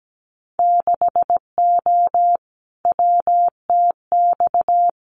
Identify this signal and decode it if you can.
Morse code, heard as a keyed tone, about 17 wpm, 700 Hz. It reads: 6O WTX
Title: Winter Field Day Exchanges Encore 17wpm